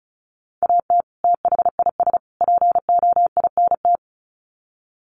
Morse code, heard as a keyed tone, about 34 wpm, 700 Hz. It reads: AT THIS POINT
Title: Sets of 3 Words 35wpm